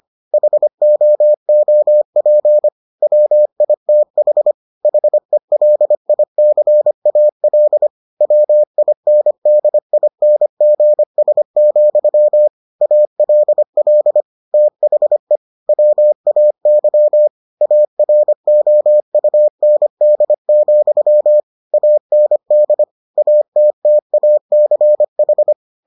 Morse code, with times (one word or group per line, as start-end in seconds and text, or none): 0.3-2.7 HOOP
3.0-4.5 WITH
4.8-7.9 HELICAL
8.2-12.5 WINDINGS,
12.8-14.2 ALL
14.5-15.4 THE
15.7-17.3 WAY
17.6-21.4 AROUND,
21.7-22.8 AND
23.2-25.5 ATTACH